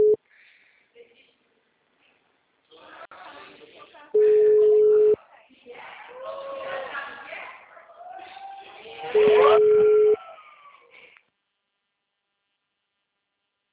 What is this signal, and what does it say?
Ringback tone (Europe, 425 Hz), with speech.
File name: out- -2006- - - .30280